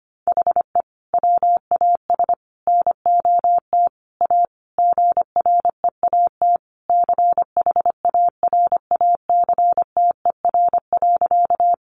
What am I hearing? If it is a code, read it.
Morse code, heard as a keyed tone, about 25 wpm, 700 Hz. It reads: HE WAS NOT A GREAT CHARACTER.
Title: Sentences from Top 1200 Words 25wpm